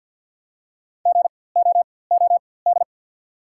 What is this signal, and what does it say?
Morse code, heard as a keyed tone, about 50 wpm, spaced as for 23 wpm, 690 Hz. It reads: GOOD
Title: 4x Character Spacing - Top 500 Words - 50wpm